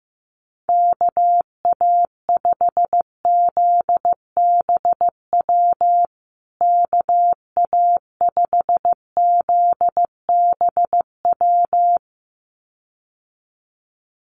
Morse code, heard as a keyed tone, about 15 wpm, 700 Hz. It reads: KA5ZBW KA5ZBW
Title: USA Call Signs Repeated 2x 15wpm